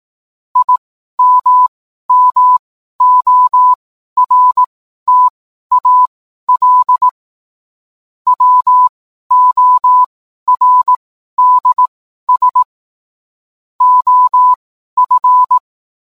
Morse code, heard as a keyed tone, about 18 wpm, spaced as for 12 wpm, 1000 Hz. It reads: IMMORTAL WORDS OF